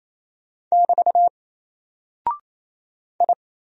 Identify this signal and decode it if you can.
Morse code, heard as a keyed tone, about 27 wpm, 700 Hz. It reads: <BT> I
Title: Single Character (Letter:Number:Punctuation:Prosign) 28wpm